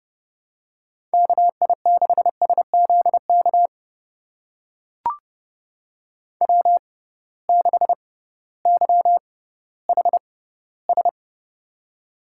Morse code, heard as a keyed tone, about 31 wpm, spaced as for 16 wpm, 700 Hz. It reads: KI6SZK W6YHS
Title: USA Call Signs - 6x Character Spacing - 30wpm